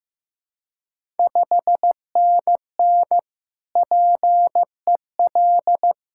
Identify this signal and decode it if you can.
Morse code, heard as a keyed tone, about 15 wpm, 700 Hz. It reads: 5NN PEL
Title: Ontario QSO Party Exchange 15wpm